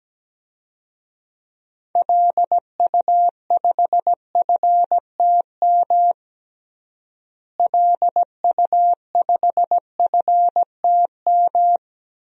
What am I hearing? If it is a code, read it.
Morse code, heard as a keyed tone, about 17 wpm, 700 Hz. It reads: LU5FTM LU5FTM
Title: DXCC-100 Call Signs - 6 Characters - Repeated 2x - 17wpm